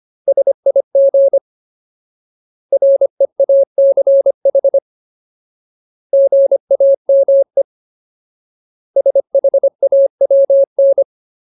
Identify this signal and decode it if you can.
Morse code, heard as a keyed tone, about 25 wpm, 550 Hz. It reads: SIG REACH GAME SHAWN
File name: Ultimate Word List / 25WPM Ultimate